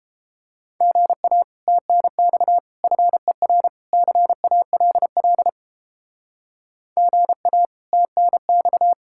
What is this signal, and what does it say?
Morse code, heard as a keyed tone, about 31 wpm, 700 Hz. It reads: GA TNX FER CALL GA TNX